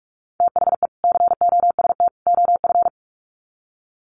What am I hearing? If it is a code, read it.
Morse code, heard as a keyed tone, about 44 wpm, 700 Hz. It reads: THE COST OF
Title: Sets of 3 Words 45wpm